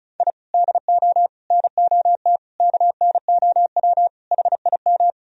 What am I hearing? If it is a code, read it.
Morse code, heard as a keyed tone, about 36 wpm, 700 Hz. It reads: I DO NOT KNOW HIM
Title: Sentences from Top 1000 Words - Review Part 1 - 35wpm